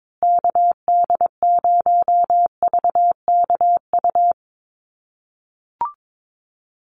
Morse code, heard as a keyed tone, about 22 wpm, 700 Hz. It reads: KD0VKU
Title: USA Call Signs Repeated 2x 22wpm